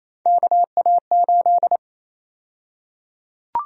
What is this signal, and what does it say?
Morse code, heard as a keyed tone, about 28 wpm, 700 Hz. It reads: KA8
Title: USA Call Sign Prefixes - Repeated 2x - 28wpm